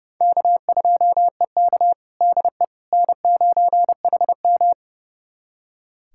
Morse code, heard as a keyed tone, about 30 wpm, 700 Hz. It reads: K2EK DE N9HM
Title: USA Callsign DE Callsign - 4 Characters - 30wpm